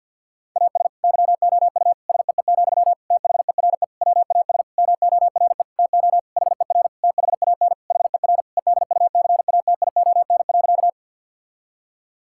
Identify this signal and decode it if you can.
Morse code, heard as a keyed tone, about 50 wpm, 700 Hz. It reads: AS YOU SEE, THERE WAS MORE TO HER THAN HER EDUCATION.
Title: Sentences from Top 1000 Words - Review Part 6 - 50wpm